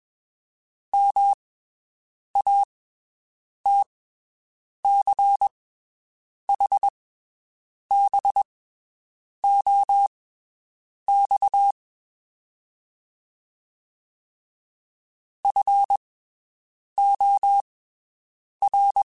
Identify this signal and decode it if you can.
Morse code, heard as a keyed tone, about 21 wpm, spaced as for 7 wpm, 800 Hz. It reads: MATCHBOX FOR
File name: (spaced) / QSOs Spaced-mp3s / WD0FBT 1 @20 sp